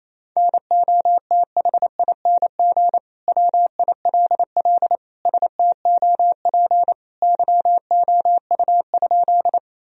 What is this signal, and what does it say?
Morse code, heard as a keyed tone, about 28 wpm, 700 Hz. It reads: NOTHING WILL STOP YOU?
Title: Sentences from Top 1000 Words - Review Part 1 - 28wpm